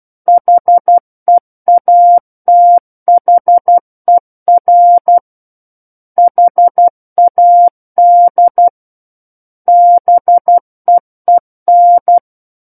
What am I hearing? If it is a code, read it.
Morse code, heard as a keyed tone, about 12 wpm, 700 Hz. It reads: HEATHER HAD BEEN
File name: Book One / 03-Chapter 3 - 12 WPM